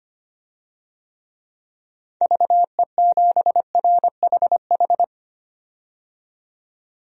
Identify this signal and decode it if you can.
Morse code, heard as a keyed tone, about 25 wpm, 690 Hz. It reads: VE7RHH
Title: DXCC-100 Call Signs - 6 Characters - 25wpm